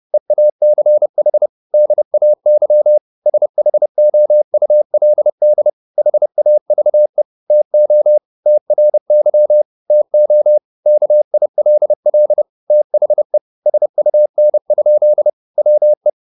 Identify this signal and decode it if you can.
Morse code, heard as a keyed tone, about 30 wpm, 600 Hz. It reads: EACH DAY SHOULD HAVE TO TRY TO KILL THE SUN? WE